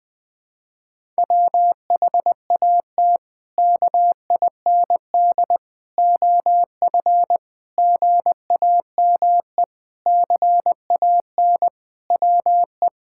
Morse code, heard as a keyed tone, about 20 wpm, 700 Hz. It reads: WHAT KIND OF GAME CAN WE
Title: Sentences from Top 1000 Words - Review Part 6 - 20wpm